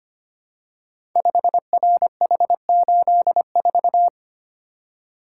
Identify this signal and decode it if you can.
Morse code, heard as a keyed tone, about 25 wpm, 700 Hz. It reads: 5RH84
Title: Code Groups - 5 Letter:Numbers - 25wpm